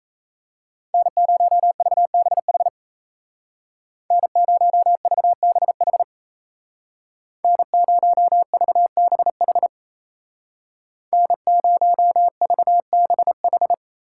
Morse code, heard as a keyed tone, about 36 wpm, 700 Hz. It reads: N0VBH N0VBH N0VBH N0VBH